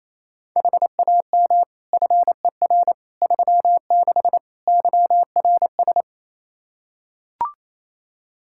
Morse code, heard as a keyed tone, about 28 wpm, 700 Hz. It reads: HAM FER 36 YRS
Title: K1USN SST QSO Practice 28wpm